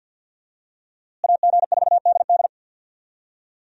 Morse code, heard as a keyed tone, about 50 wpm, 700 Hz. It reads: AG4DD
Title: USA Call Signs - 5 Characters - Repeated 3x - 50wpm